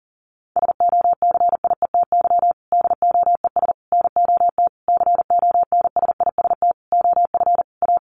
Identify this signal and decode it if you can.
Morse code, heard as a keyed tone, about 37 wpm, 700 Hz. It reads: SOCIETY DOES NOT CONSIST OF A
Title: Sentences from Top 1400 Words 40wpm